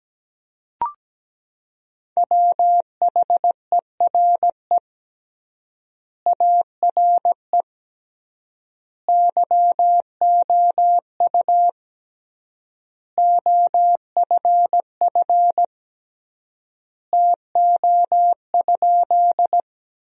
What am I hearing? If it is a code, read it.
Morse code, heard as a keyed tone, about 17 wpm, 700 Hz. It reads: WHERE ARE YOU OFF TO?
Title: 3x Word-Spacing - Sentences from Top 200 Words - 17wpm